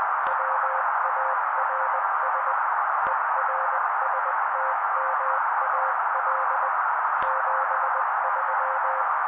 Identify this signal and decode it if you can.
Morse code, heard as a keyed tone, about 20 wpm, 530 Hz. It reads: WARS ERSTMAL 73
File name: daily / 18wpm noise